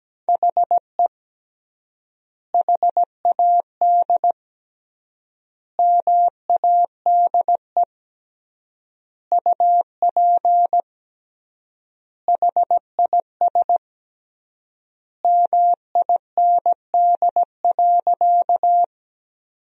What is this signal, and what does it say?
Morse code, heard as a keyed tone, about 17 wpm, 700 Hz. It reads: HE HAD MADE UP HIS MIND.
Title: 3x Word-Spacing - Sentences from Top 500 Words - 17wpm